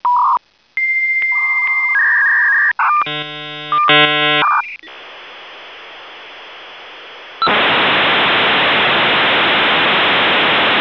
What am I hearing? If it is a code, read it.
Dial-up modem handshake.
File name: modem / v90softmot